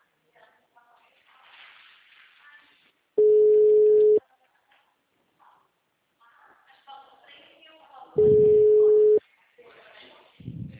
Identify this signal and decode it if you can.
Ringback tone (Europe, 425 Hz), with speech.
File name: out- -2001- - - .77575